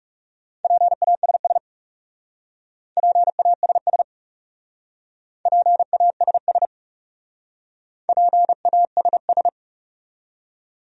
Morse code, heard as a keyed tone, about 38 wpm, 700 Hz. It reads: PASS PASS PASS PASS